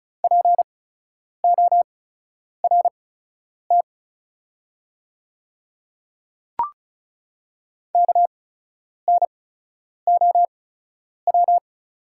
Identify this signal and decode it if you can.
Morse code, heard as a keyed tone, about 36 wpm, spaced as for 10 wpm, 700 Hz. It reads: PORT KNOW
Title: 8x Character Spacing - Top 300 Words - 35wpm